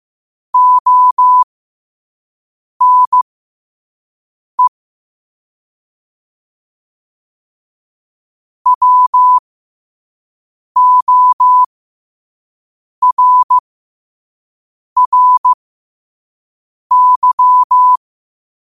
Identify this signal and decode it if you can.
Morse code, heard as a keyed tone, about 15 wpm, spaced as for 5 wpm, 1000 Hz. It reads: ONE WORRY
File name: qotdfiles / 5wpm